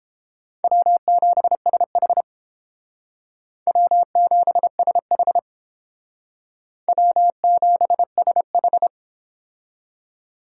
Morse code, heard as a keyed tone, about 31 wpm, 700 Hz. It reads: W7SH W7SH W7SH